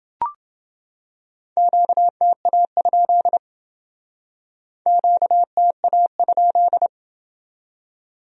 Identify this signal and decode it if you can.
Morse code, heard as a keyed tone, about 28 wpm, 700 Hz. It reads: QTA? QTA?